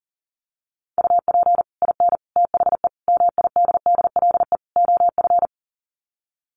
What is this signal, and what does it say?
Morse code, heard as a keyed tone, about 40 wpm, 700 Hz. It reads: UP IN THE MIDDLE OF
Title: Sets of 5 Words Encore 40wpm